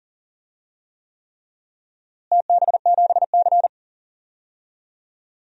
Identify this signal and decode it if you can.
Morse code, heard as a keyed tone, about 40 wpm, 700 Hz. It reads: TB7C
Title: DXCC-100 Call Signs - 4 Characters - Repeated 3x - 40wpm